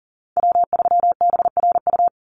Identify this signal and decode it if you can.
Morse code, heard as a keyed tone, about 40 wpm, 700 Hz. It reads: W3BRU
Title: USA Call Signs 40wpm